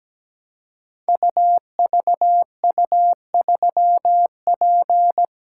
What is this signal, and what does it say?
Morse code, heard as a keyed tone, about 17 wpm, 700 Hz. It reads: UVU3P
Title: Code Groups - 5 Letter:Numbers - 17wpm